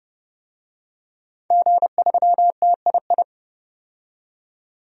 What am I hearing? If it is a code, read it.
Morse code, heard as a keyed tone, about 30 wpm, 700 Hz. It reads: G3TII
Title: DXCC-100 Call Signs - 5 Characters - 30wpm